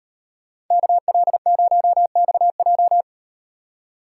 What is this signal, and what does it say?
Morse code, heard as a keyed tone, about 38 wpm, 700 Hz. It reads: KL0XJ